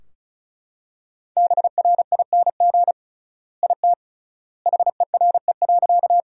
Morse code, heard as a keyed tone, about 36 wpm, 700 Hz. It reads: BRING IT HERE.
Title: 3x Word-Spacing - Sentences from Top 500 Words - 35wpm